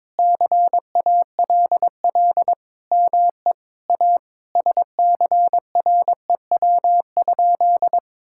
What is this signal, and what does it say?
Morse code, heard as a keyed tone, about 22 wpm, 700 Hz. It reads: CALL ME A SCREW?
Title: Sentences from Top 1600 Words 22wpm